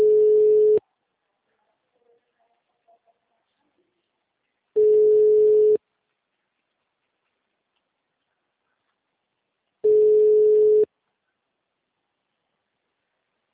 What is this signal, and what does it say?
Ringback tone (Europe, 425 Hz).